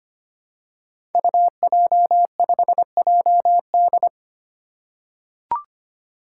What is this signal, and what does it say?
Morse code, heard as a keyed tone, about 25 wpm, 700 Hz. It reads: UJ5JD
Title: Code Groups - 5 Letter:Numbers - 25wpm